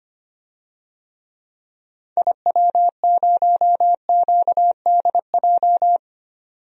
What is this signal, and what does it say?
Morse code, heard as a keyed tone, about 25 wpm, 700 Hz. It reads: IW0QDJ
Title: DXCC-100 Call Signs - 6 Characters - 25wpm